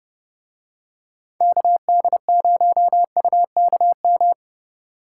Morse code, heard as a keyed tone, about 28 wpm, 700 Hz. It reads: KD0UKM